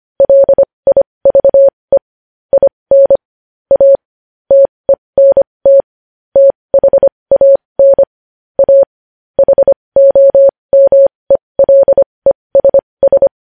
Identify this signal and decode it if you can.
Morse code, heard as a keyed tone, about 25 wpm, spaced as for 20 wpm, 550 Hz. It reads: LIVE IN A TENT THAN A HOMELESS
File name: podcast / News headlines-1x-25x20